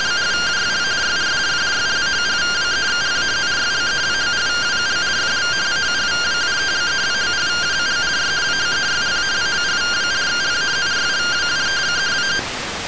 Two-tone teleprinter signal, about 170 Hz shift.